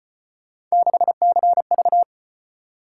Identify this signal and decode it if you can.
Morse code, heard as a keyed tone, about 35 wpm, 700 Hz. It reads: 6CV